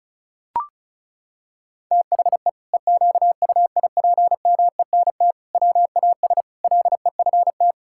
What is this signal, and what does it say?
Morse code, heard as a keyed tone, about 34 wpm, 700 Hz. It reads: THE EQUIPMENT WAS LEFT
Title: Sentences from Top 1000 Words - Review Part 5 - 35wpm